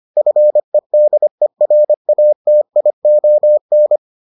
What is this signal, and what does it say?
Morse code, heard as a keyed tone, about 25 wpm, 600 Hz. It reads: FEDERATION